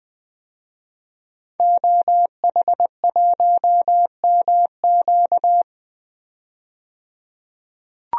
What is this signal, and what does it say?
Morse code, heard as a keyed tone, about 20 wpm, 700 Hz. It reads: OH1MQ
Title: DXCC-100 Call Signs - 5 Characters - 20wpm